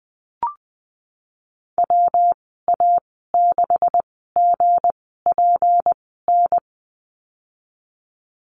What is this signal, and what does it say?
Morse code, heard as a keyed tone, about 20 wpm, spaced as for 15 wpm, 700 Hz. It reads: WA6GPN
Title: 179 Lesson 29g - Slash - Call Signs 20wpm